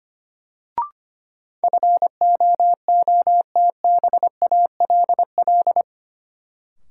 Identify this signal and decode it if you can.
Morse code, heard as a keyed tone, about 25 wpm, 700 Hz. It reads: FOOTBALL
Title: Sentences from Top 1600 Words - New Word Drill Alt - 25wpm